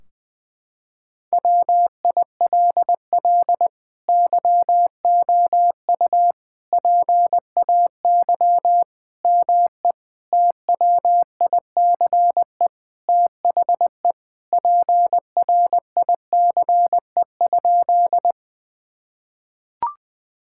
Morse code, heard as a keyed tone, about 19 wpm, 700 Hz. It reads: WILL YOU PAY ME TWICE THE PRICE?
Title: Sentences from Top 1600 Words 20wpm